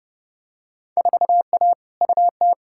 Morse code, heard as a keyed tone, about 31 wpm, 700 Hz. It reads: 4A UT
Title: Field Day Exchanges - 30wpm